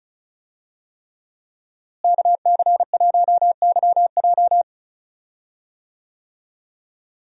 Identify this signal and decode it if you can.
Morse code, heard as a keyed tone, about 36 wpm, 700 Hz. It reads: KC1YJ
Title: USA Call Signs - 5 Characters - Repeated 3x - 35wpm